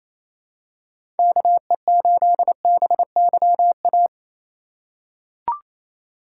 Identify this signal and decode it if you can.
Morse code, heard as a keyed tone, about 28 wpm, 700 Hz. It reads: KE8BYA
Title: USA Call Signs - 6 Characters - 3x Character Spacing - 28wpm